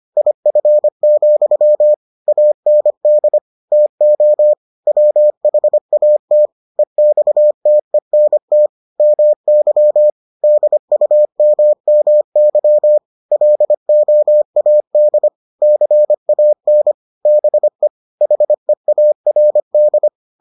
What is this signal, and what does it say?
Morse code, heard as a keyed tone, about 25 wpm, 600 Hz. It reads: IF, AND TO WHAT EXTENT MY DUMMY LOAD CAN BE HEARD